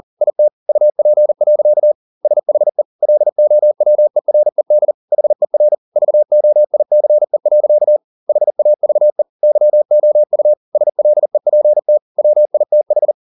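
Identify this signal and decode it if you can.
Morse code, heard as a keyed tone, about 40 wpm, 600 Hz. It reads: IT UP. SHE LOWERED HER VOICE. HAVE YOU SLEPT WITH
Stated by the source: Morse code at 40 wpm, practice text, CW med Gnister.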